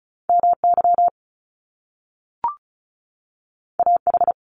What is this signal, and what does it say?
Morse code, heard as a keyed tone, about 36 wpm, 700 Hz. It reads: MY AH
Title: Top Two Letter Combinations in Words 35wpm